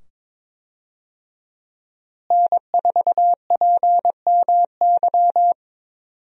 Morse code, heard as a keyed tone, about 22 wpm, 700 Hz. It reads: N4PMY
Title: USA Call Signs - 5 Characters - Repeated 2x - 22wpm